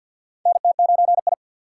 Morse code, heard as a keyed tone, about 46 wpm, 700 Hz. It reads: NT9I
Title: USA Call Signs - 4 Characters - 3x Character Spacing - 50wpm